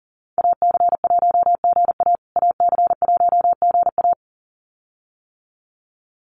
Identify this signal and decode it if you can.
Morse code, heard as a keyed tone, about 40 wpm, 700 Hz. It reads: AC1GA AC1GA
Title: USA Call Signs Repeated 2x 40wpm